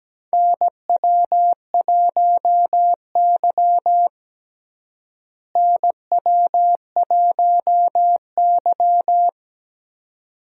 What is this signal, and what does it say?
Morse code, heard as a keyed tone, about 17 wpm, 700 Hz. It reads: NW1Y NW1Y
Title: USA Call Signs - 4 Characters - Repeated 2x - 17wpm